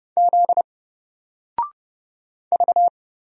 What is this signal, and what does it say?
Morse code, heard as a keyed tone, about 30 wpm, 700 Hz. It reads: Z V
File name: Single-Letter-Number-30wpm